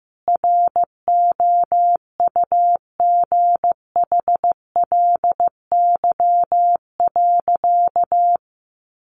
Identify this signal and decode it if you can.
Morse code, heard as a keyed tone, about 15 wpm, 700 Hz. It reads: ROUGHLY.
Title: Sentences from Top 1500 Words 15wpm